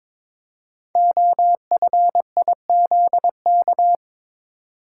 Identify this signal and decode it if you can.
Morse code, heard as a keyed tone, about 22 wpm, 700 Hz. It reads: OFIZK
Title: Code Groups - 5 Letters - 22wpm